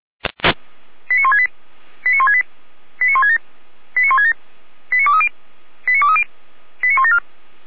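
Five-tone selective call (ZVEI), heard as quick runs of five tones, tones of about 80 ms. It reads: 89168 89168 89167 89167 89230 89230 89174
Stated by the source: ZVEI five-tone call 89160 received by SDR, recorded by fire brigade Feuerwehr Orlamünde.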